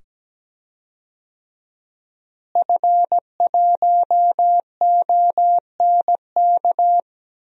Morse code, heard as a keyed tone, about 17 wpm, 700 Hz. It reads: F1ONK
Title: DXCC-100 Call Signs - 5 Characters - Repeated 2x - 17wpm